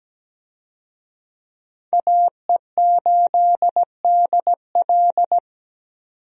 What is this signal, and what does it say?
Morse code, heard as a keyed tone, about 17 wpm, 700 Hz. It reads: AE8DL
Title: USA Call Signs - 5 Characters - 17wpm